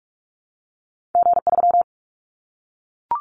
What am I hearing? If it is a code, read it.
Morse code, heard as a keyed tone, about 44 wpm, 700 Hz. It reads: G3
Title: Callsign Prefixes for Top 25 DXCC - Repeated 3x - 45wpm